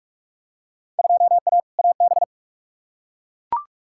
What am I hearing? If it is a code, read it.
Morse code, heard as a keyed tone, about 43 wpm, 700 Hz. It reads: 2A AB
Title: Field Day Exchanges - 45wpm